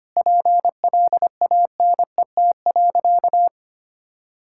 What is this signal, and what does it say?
Morse code, heard as a keyed tone, about 25 wpm, 700 Hz. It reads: PLANET.
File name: practice / 25wpm / Difficult Sentences with Hints - 25wpm